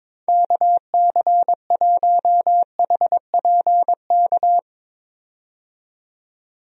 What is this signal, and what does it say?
Morse code, heard as a keyed tone, about 21 wpm, 700 Hz. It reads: KC1HPK
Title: USA Call Signs - 6 Characters - 22wpm